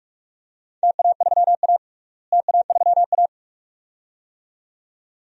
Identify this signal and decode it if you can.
Morse code, heard as a keyed tone, about 46 wpm, 700 Hz. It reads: TA3A TA3A
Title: DXCC-100 Call Signs - 4 Characters - Repeated 2x - 45wpm